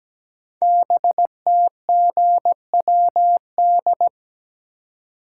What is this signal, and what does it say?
Morse code, heard as a keyed tone, about 17 wpm, 700 Hz. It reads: BTGWD
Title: Code Groups - 5 Letters - 17wpm